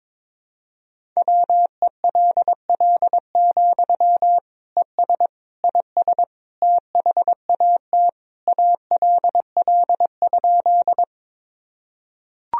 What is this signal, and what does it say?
Morse code, heard as a keyed tone, about 22 wpm, 700 Hz. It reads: WELL, ES IS THAT ALL?
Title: Sentences from Top 200 Words and Abbreviations 22wpm